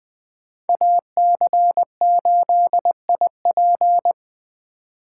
Morse code, heard as a keyed tone, about 20 wpm, 700 Hz. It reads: AC8IP